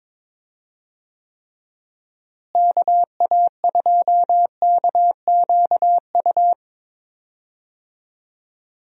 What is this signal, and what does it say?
Morse code, heard as a keyed tone, about 22 wpm, 700 Hz. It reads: KA2KQU
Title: USA Call Signs - 6 Characters - Repeated 2x - 22wpm